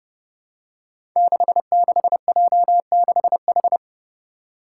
Morse code, heard as a keyed tone, about 30 wpm, 700 Hz. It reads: 66J6H
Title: Five Challenging Letter-Number Combinations 30wpm